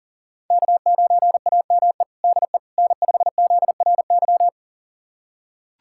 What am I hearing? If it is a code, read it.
Morse code, heard as a keyed tone, about 40 wpm, 700 Hz. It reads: K9AME DE N5ZRY
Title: USA Callsign DE Callsign - 5 Characters - 40wpm